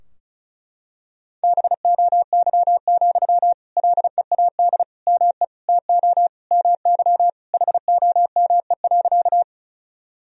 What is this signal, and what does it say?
Morse code, heard as a keyed tone, about 36 wpm, 700 Hz. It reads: BOY, LEAD ME TO MY HOME.
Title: Sentences from Top 1000 Words - Review Part 2 - 35wpm